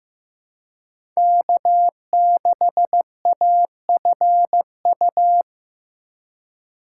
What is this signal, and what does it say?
Morse code, heard as a keyed tone, about 15 wpm, 700 Hz. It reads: K6AFU
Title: USA Call Signs - 5 Characters - 2x Character Spacing - 15wpm